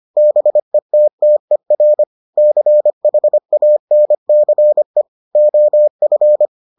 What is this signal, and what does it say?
Morse code, heard as a keyed tone, about 25 wpm, 600 Hz. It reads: BETTER CHANCE OF